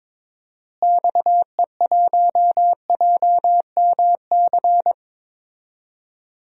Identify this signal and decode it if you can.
Morse code, heard as a keyed tone, about 22 wpm, 700 Hz. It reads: XE1JMC
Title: DXCC-100 Call Signs - 6 Characters - Repeated 2x - 22wpm